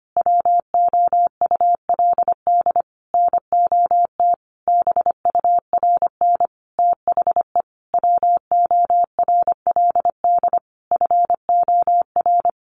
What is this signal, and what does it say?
Morse code, heard as a keyed tone, about 25 wpm, 700 Hz. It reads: WOULD NOT BURN THE WORLD FOR
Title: Sentences from Top 1300 Words 25wpm